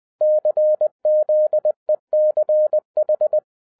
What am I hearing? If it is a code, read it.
Morse code, heard as a keyed tone, about 20 wpm, 610 Hz. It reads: CZECH
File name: 2011-07-mp3 / n43 Czech Army Morse Keys 20wpm